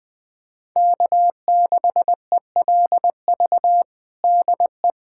Morse code, heard as a keyed tone, about 19 wpm, 700 Hz. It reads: K6ELV DE
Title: USA Callsign DE Callsign - 5 Characters - 20wpm